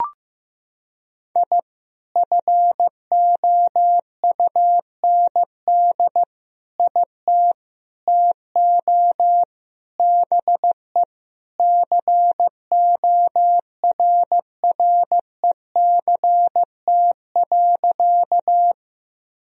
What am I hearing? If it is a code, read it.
Morse code, heard as a keyed tone, about 16 wpm, 700 Hz. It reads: I FOUND IT TO BE CORRECT.
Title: Sentences from Top 1000 Words - Review Part 2 - 15wpm